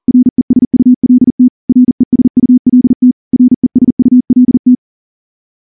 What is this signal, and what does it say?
Morse code, heard as a keyed tone, about 38 wpm, 250 Hz. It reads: RESULT RESULT RESULT
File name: morse-training-40WPM